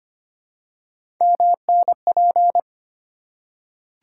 Morse code, heard as a keyed tone, about 25 wpm, 700 Hz. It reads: MNP
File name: Three Letter-Number Combinations - 25wpm